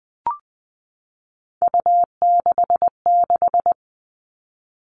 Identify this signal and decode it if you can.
Morse code, heard as a keyed tone, about 20 wpm, 700 Hz. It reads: U66
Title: Three Challenging Letter-Number Combinations 20wpm